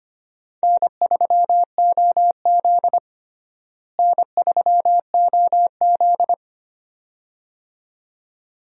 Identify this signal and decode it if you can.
Morse code, heard as a keyed tone, about 25 wpm, 700 Hz. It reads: N3OZ N3OZ
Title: USA Call Signs - 4 Characters - Repeated 2x - 25wpm